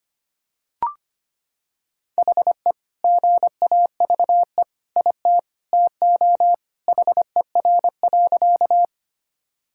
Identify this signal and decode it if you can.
Morse code, heard as a keyed tone, about 25 wpm, 700 Hz. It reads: HE GAVE IT TO HER.
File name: practice / 25wpm / Sentences with 2-Word Phrases - 25wpm